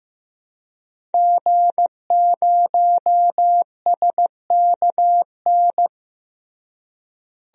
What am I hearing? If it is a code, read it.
Morse code, heard as a keyed tone, about 15 wpm, 700 Hz. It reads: G0SKN
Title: DXCC-100 Call Signs - 5 Characters - Repeated 2x - 15wpm